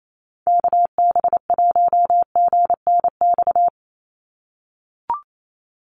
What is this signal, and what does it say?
Morse code, heard as a keyed tone, about 28 wpm, 700 Hz. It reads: KB1GNX
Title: USA Call Signs 28wpm